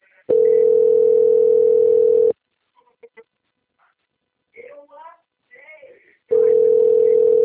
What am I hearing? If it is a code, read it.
Ringback tone (North America), 440 + 480 Hz, with speech.
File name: out- - - - - .686